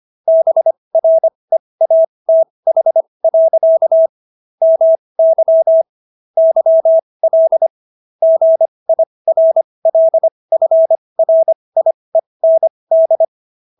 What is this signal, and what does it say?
Morse code, heard as a keyed tone, about 25 wpm, spaced as for 20 wpm, 650 Hz. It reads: BREATH. MY YL GIRLFRIEND